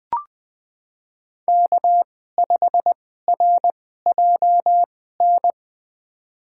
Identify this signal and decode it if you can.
Morse code, heard as a keyed tone, about 20 wpm, spaced as for 14 wpm, 700 Hz. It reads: K5RJN
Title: USA Call Signs - 5 Characters - 2x Character Spacing - 20wpm